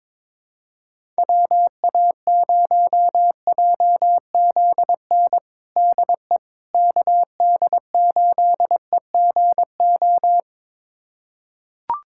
Morse code, heard as a keyed tone, about 22 wpm, 700 Hz. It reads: WA0JZN DE KD8EGO
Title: USA Callsign DE Callsign 22wpm